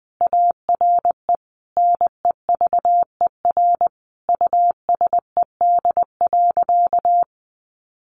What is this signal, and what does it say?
Morse code, heard as a keyed tone, about 20 wpm, 700 Hz. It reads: ARE NEVER USED.
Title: Sentences from Top 600 Words 20wpm